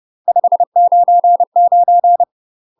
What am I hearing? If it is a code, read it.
Morse code, heard as a keyed tone, about 30 wpm, 700 Hz. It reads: 599
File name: UK MASTER QSOs TRAINING/30WPM